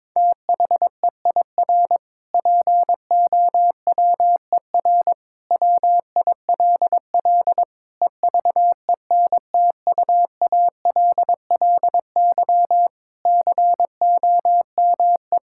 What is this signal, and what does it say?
Morse code, heard as a keyed tone, about 21 wpm, 700 Hz. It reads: THEIR POWER WILL EVENTUALLY COME
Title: Sentences from Top 1000 Words - Review Part 6 - 22wpm